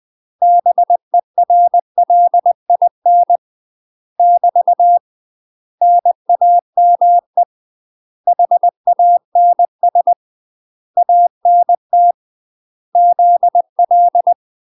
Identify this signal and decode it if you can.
Morse code, heard as a keyed tone, about 20 wpm, 700 Hz. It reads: BERLIN <BT> NAME HANS ANT ZL
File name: UK MASTER QSOs TRAINING/20WPM